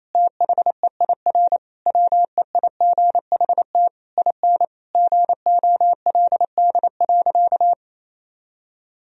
Morse code, heard as a keyed tone, about 27 wpm, 700 Hz. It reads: THEIR WEIGHT IN GOLD.
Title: Sentences from Top 1000 Words - Review Part 3 - 28wpm